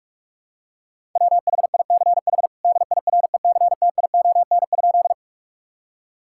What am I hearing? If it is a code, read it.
Morse code, heard as a keyed tone, about 46 wpm, 700 Hz. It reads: WHICH DIRECTION?
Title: Sentences from Top 1000 Words - Review Part 5 - 45wpm